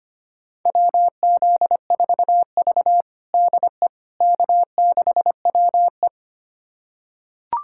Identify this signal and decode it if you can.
Morse code, heard as a keyed tone, about 25 wpm, 700 Hz. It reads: WZ4V DE K6WE
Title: USA Callsign DE Callsign - 4 Characters - 25wpm